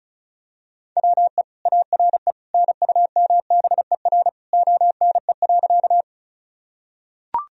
Morse code, heard as a keyed tone, about 36 wpm, 700 Hz. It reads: WE ARE NUMBER ONE.
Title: Sentences from Top 1000 Words - Review Part 1 - 35wpm